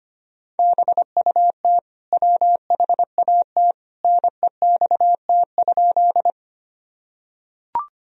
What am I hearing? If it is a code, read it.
Morse code, heard as a keyed tone, about 25 wpm, 700 Hz. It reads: BUT WHAT NEXT?
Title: Sentences from Top 1000 Words - Review Part 1 - 25wpm